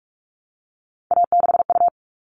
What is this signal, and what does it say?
Morse code, heard as a keyed tone, about 44 wpm, 700 Hz. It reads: A6U
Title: All Cut Numbers - 3-Digit 45wpm